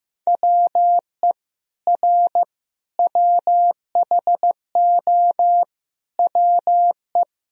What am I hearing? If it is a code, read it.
Morse code, heard as a keyed tone, about 15 wpm, 700 Hz. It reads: WE R WHO WE
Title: Sentences from Top 200 Words and Abbreviations 15wpm